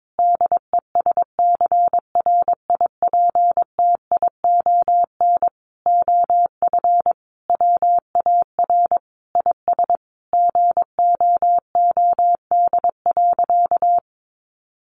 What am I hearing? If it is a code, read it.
Morse code, heard as a keyed tone, about 22 wpm, 700 Hz. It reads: DESCRIPTION OF WAR IS GOOD.
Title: Sentences from Top 1300 Words 22wpm